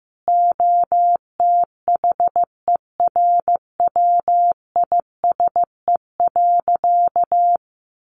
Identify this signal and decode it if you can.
Morse code, heard as a keyed tone, about 15 wpm, 700 Hz. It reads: OTHERWISE.
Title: Sentences from Top 1200 Words 15wpm